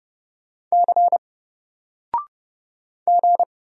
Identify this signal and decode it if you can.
Morse code, heard as a keyed tone, about 30 wpm, 700 Hz. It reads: C G
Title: Single Letter-Number Alt 30wpm